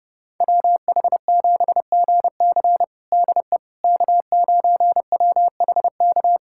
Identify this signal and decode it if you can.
Morse code, heard as a keyed tone, about 29 wpm, 700 Hz. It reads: WH7GC DE K9WHK
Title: USA Callsign DE Callsign - 5 Characters - 30wpm